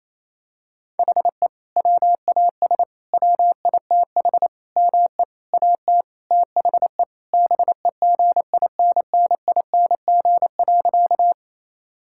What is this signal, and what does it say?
Morse code, heard as a keyed tone, about 27 wpm, 700 Hz. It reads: HE WAS WITH ME AT THE BEGINNING.
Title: Sentences from Top 1000 Words - Review Part 5 - 28wpm